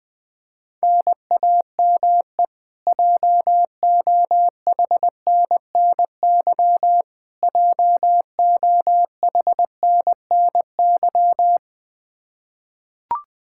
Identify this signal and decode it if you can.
Morse code, heard as a keyed tone, about 20 wpm, 700 Hz. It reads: NAME JOHNNY JOHNNY
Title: Practice QSO - Exchange 1 - Beginning - 20wpm